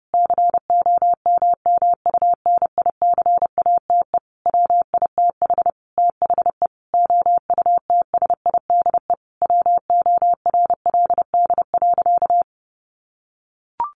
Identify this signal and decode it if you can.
Morse code, heard as a keyed tone, about 30 wpm, 700 Hz. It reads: COMMUNICATE WITH THE OUTSIDE WORLD.
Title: Sentences from Top 1300 Words 30wpm